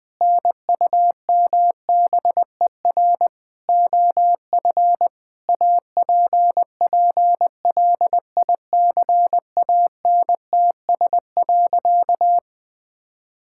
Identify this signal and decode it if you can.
Morse code, heard as a keyed tone, about 21 wpm, 700 Hz. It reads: NUMBER OF APPLICANTS.
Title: Sentences with 6-Word Phrases - 20wpm